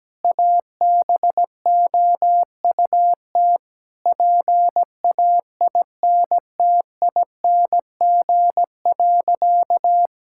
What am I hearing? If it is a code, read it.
Morse code, heard as a keyed tone, about 17 wpm, 700 Hz. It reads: ABOUT PAINTING.
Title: Sentences from Top 1000 Words - Review Part 6 - 17wpm